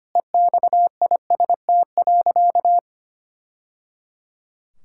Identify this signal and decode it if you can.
Morse code, heard as a keyed tone, about 25 wpm, 700 Hz. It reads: EXIST.
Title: Sentences from Top 1000 Words - Review Part 5 - 25wpm